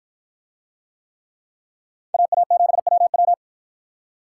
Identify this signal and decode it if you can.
Morse code, heard as a keyed tone, about 53 wpm, 700 Hz. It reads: AA7WW